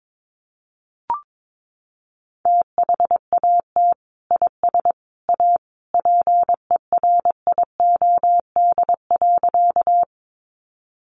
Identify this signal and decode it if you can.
Morse code, heard as a keyed tone, about 22 wpm, 700 Hz. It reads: THAT IS A PERIOD.
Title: Sentences from Top 800 Words 22wpm